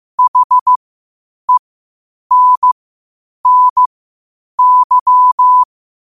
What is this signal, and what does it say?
Morse code, heard as a keyed tone, about 15 wpm, spaced as for 8 wpm, 1000 Hz. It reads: HENNY
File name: qotdfiles / 8wpm